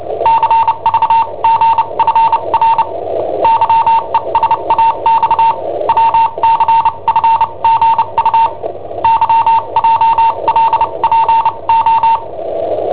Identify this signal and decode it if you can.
Morse code, heard as a keyed tone, about 29 wpm, spaced as for 25 wpm, 910 Hz. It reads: CVGFR YESAX WCFGU YJLPO
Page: Cassetta esercizi preparata da I7OHP per IW5ELP